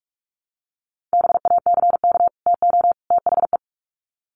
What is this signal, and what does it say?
Morse code, heard as a keyed tone, about 44 wpm, 700 Hz. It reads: BACK TO THE
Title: Sets of 3 Words 45wpm